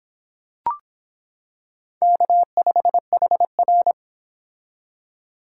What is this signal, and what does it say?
Morse code, heard as a keyed tone, about 26 wpm, 700 Hz. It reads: K5HR